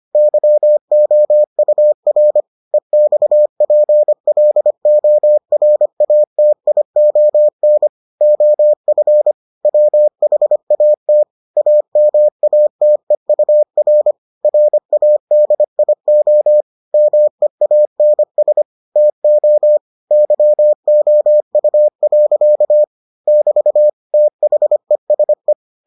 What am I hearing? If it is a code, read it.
Morse code, heard as a keyed tone, about 25 wpm, 600 Hz. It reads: YOUR EXPLORATION OF WHAT AMATEUR RADIO MEANS TO YOU. <BT> THESE